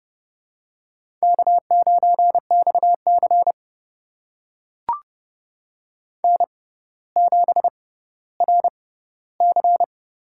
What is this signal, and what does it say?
Morse code, heard as a keyed tone, about 30 wpm, spaced as for 15 wpm, 700 Hz. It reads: K9XC N7RC
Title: USA Call Signs - 4 Characters - 6x Character Spacing - 30wpm